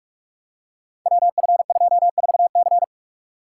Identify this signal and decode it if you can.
Morse code, heard as a keyed tone, about 44 wpm, 700 Hz. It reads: WF24C
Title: Code Groups - 5 Letter:Numbers - 45wpm